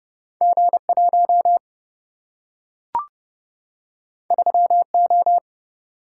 Morse code, heard as a keyed tone, about 29 wpm, 700 Hz. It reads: G1 3O
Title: Single-Letter Single-Number 30wpm